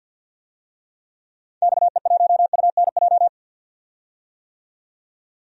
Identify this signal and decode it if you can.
Morse code, heard as a keyed tone, about 46 wpm, 700 Hz. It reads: XE1UNJ
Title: DXCC-100 Call Signs - 6 Characters - 50wpm